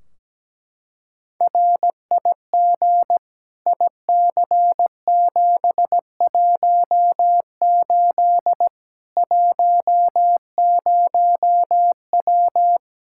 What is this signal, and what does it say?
Morse code, heard as a keyed tone, about 17 wpm, 700 Hz. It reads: RIG IC718 10W
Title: Practice QSO - Exchange 2 - Info - 17wpm